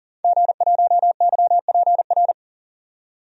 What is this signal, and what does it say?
Morse code, heard as a keyed tone, about 40 wpm, 700 Hz. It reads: G1YPR
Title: DXCC-100 Call Signs - 5 Characters - 40wpm